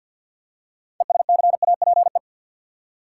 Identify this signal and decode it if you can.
Morse code, heard as a keyed tone, about 53 wpm, 700 Hz. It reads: ESCAPE
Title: Six Letter Words - 50wpm